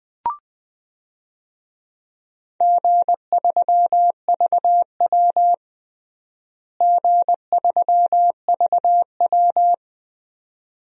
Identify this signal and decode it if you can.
Morse code, heard as a keyed tone, about 20 wpm, 700 Hz. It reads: G3VW G3VW
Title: DXCC-100 Call Signs - 4 Characters - Repeated 2x - 20wpm